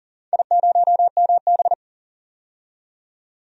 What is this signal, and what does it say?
Morse code, heard as a keyed tone, about 40 wpm, 700 Hz. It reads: I0MB
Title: DXCC-100 Call Signs - 4 Characters - 40wpm